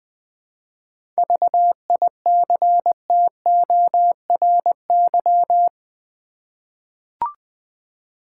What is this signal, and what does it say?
Morse code, heard as a keyed tone, about 20 wpm, 700 Hz. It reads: VICTORY
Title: 7 Letter Words - First 3 Letters Given - 20wpm